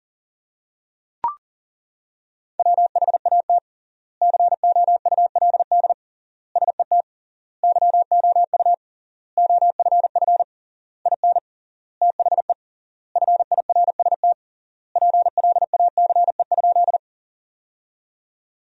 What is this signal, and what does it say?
Morse code, heard as a keyed tone, about 40 wpm, 700 Hz. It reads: WHAT COULD SET YOU OFF IN THE FIRST PLACE?
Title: 3x Word-Spacing - Sentences from Top 200 Words - 40wpm